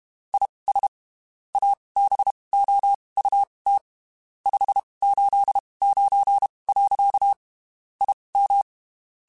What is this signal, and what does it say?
Morse code, heard as a keyed tone, about 32 wpm, spaced as for 23 wpm, 800 Hz. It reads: IS ABOUT 589. IM
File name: (spaced) / QSOs Spaced-mp3s / WD0FBT 1 @30 sp